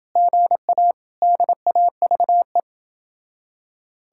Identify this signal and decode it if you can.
Morse code, heard as a keyed tone, about 29 wpm, 700 Hz. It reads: GA DAVE